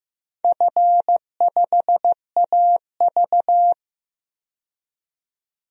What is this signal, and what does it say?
Morse code, heard as a keyed tone, about 15 wpm, 700 Hz. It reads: F5AV
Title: DXCC-100 Call Signs - 4 Characters - 15wpm